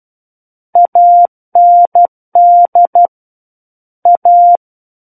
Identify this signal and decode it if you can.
Morse code, heard as a keyed tone, about 12 wpm, 700 Hz. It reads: AND A
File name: Book One / 03-Chapter 3 - 12 WPM